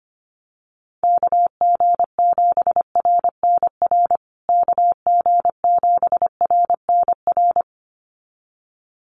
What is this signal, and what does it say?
Morse code, heard as a keyed tone, about 25 wpm, 700 Hz. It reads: KG7RNR KG7RNR
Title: USA Call Signs Repeated 2x 25wpm